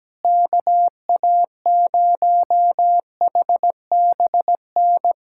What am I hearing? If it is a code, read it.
Morse code, heard as a keyed tone, about 17 wpm, 700 Hz. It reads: KA0HBN
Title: USA Callsign DE Callsign - 6 Characters - 17wpm